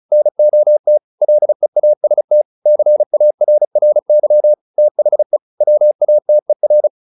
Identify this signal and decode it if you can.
Morse code, heard as a keyed tone, about 34 wpm, 600 Hz. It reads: NOT LEAST CARRY THE WATER